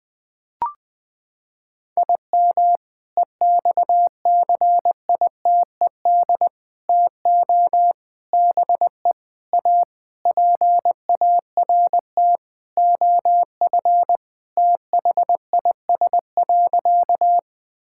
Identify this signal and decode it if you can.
Morse code, heard as a keyed tone, about 19 wpm, 700 Hz. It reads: IM EXCITED TO BE A PART OF THIS.
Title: Sentences with 5-Word Phrases - 20wpm